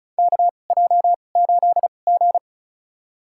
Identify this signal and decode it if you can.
Morse code, heard as a keyed tone, about 36 wpm, spaced as for 25 wpm, 700 Hz. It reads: KJ8G
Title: USA Call Signs - 2x Character Spacing - 35wpm